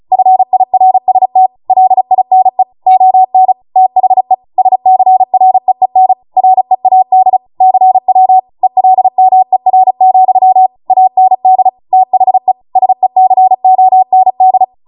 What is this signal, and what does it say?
Morse code, heard as a keyed tone, about 36 wpm, 750 Hz. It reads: FIRST LINE ON THE SCREEN READ CW ELMER, AND THE SECOND